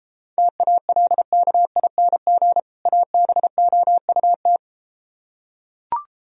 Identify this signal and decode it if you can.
Morse code, heard as a keyed tone, about 33 wpm, 700 Hz. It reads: TALKING ABOUT